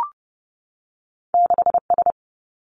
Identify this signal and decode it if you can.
Morse code, heard as a keyed tone, about 30 wpm, 700 Hz. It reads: 6 S
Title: Two Challenging Letter-Number Combinations 30wpm